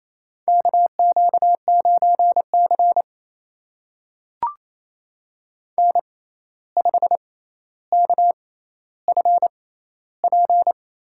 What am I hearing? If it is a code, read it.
Morse code, heard as a keyed tone, about 28 wpm, spaced as for 13 wpm, 700 Hz. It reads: KQ9C N5KFP
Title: USA Call Signs - 6x Character Spacing - 28wpm